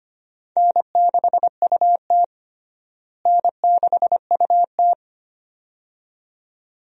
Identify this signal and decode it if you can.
Morse code, heard as a keyed tone, about 25 wpm, 700 Hz. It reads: N6UT N6UT
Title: USA Call Signs - 4 Characters - Repeated 2x - 25wpm